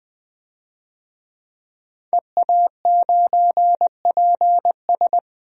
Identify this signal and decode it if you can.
Morse code, heard as a keyed tone, about 20 wpm, 700 Hz. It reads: EA9PS
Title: DXCC-100 Call Signs - 5 Characters - 20wpm